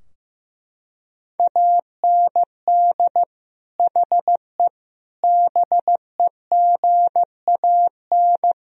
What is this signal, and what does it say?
Morse code, heard as a keyed tone, about 15 wpm, 700 Hz. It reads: AND HE BEGAN
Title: Sentences from Top 1000 Words - Review Part 4 - 15wpm